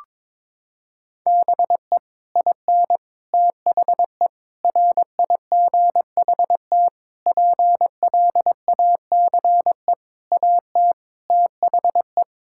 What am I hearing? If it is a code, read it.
Morse code, heard as a keyed tone, about 22 wpm, 700 Hz. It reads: BE IN THE RIGHT PLACE AT THE
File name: Sets of 9 Words - 22wpm